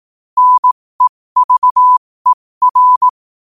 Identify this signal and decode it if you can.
Morse code, heard as a keyed tone, about 18 wpm, spaced as for 15 wpm, 1000 Hz. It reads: NEVER
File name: qotdfiles / 15wpm